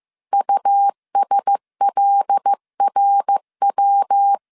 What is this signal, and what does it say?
Morse code, heard as a keyed tone, about 15 wpm, 800 Hz. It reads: USLRW
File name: CW / DET / 1 6 1500C S C 14wpm 2951